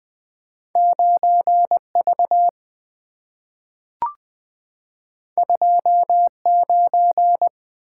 Morse code, heard as a keyed tone, about 20 wpm, 700 Hz. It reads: 9V 29
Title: Two Letter-Number Combinations - 20wpm